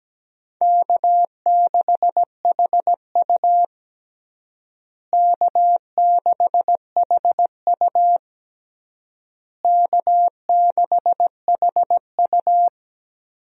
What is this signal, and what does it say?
Morse code, heard as a keyed tone, about 17 wpm, 700 Hz. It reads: K6HU K6HU K6HU
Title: USA Call Signs - 4 Characters - Repeated 3x - 17wpm